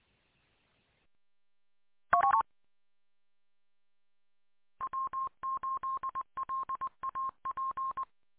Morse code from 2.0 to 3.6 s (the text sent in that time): U